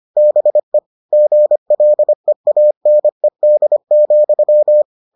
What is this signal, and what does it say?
Morse code, heard as a keyed tone, about 25 wpm, 600 Hz. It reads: BE GLEANED,